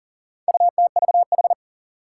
Morse code, heard as a keyed tone, about 40 wpm, 700 Hz. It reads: UTVH
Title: Four Challenging Letter-Number Combinations 40wpm